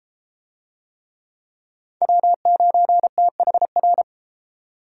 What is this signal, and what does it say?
Morse code, heard as a keyed tone, about 32 wpm, 700 Hz. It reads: W9THR